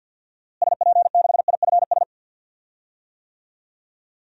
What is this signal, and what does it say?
Morse code, heard as a keyed tone, about 51 wpm, 700 Hz. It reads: SP6IFS
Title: DXCC-100 Call Signs - 6 Characters - Repeated 2x - 50wpm